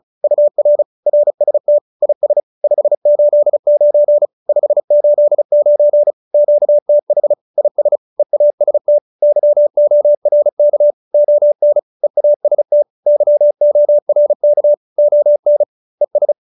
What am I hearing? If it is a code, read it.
Morse code, heard as a keyed tone, about 34 wpm, 590 Hz. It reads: UR RST IS 589 589 QTH IS EAST YORK ON EAST YORK ON ES